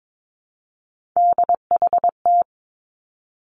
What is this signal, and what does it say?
Morse code, heard as a keyed tone, about 22 wpm, 700 Hz. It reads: DHT
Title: Three Challenging Letter-Number Combinations 22wpm